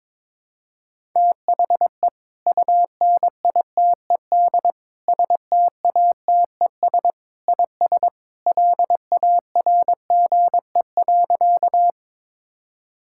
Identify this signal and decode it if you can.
Morse code, heard as a keyed tone, about 21 wpm, 700 Hz. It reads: THE UNITED STATES IS LARGE.
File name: practice / 22wpm / Sentences with 3-Word Phrases - 22wpm